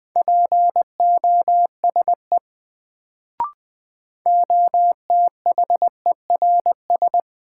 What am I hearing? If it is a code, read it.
Morse code, heard as a keyed tone, about 20 wpm, 700 Hz. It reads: POSE OTHERS
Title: Sentences from Top 1000 Words - Review Part 2 - New Word Drill - 20wpm